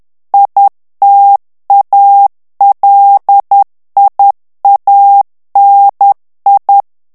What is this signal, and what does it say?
Morse code, heard as a keyed tone, about 10 wpm, 800 Hz. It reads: ITALIANI
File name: morse mp3 / ITALIEN texte clair 1 vitesse 10 mots minute